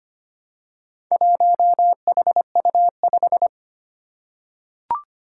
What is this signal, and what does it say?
Morse code, heard as a keyed tone, about 25 wpm, 700 Hz. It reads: 1HU5
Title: Four Challenging Letter-Number Combinations 25wpm